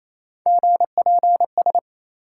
Morse code, heard as a keyed tone, about 28 wpm, 700 Hz. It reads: GPS